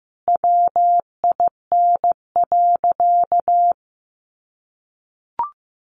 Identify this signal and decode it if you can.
Morse code, heard as a keyed tone, about 15 wpm, 700 Hz. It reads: WIN.
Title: Sentences from Top 1500 Words 15wpm